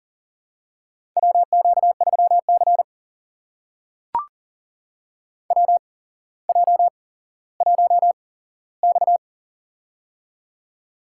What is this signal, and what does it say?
Morse code, heard as a keyed tone, about 40 wpm, spaced as for 17 wpm, 700 Hz. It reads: WQ3C WJ1X
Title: USA Call Signs - 4 Characters - 8x Character Spacing - 40wpm